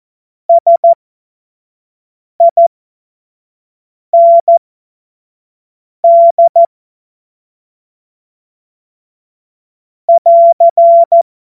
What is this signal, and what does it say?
Morse code, heard as a keyed tone, about 14 wpm, spaced as for 5 wpm, 680 Hz. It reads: SIND <AR>
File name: Morsekurs / Witze / Witz 23. Mai 2023 25 F70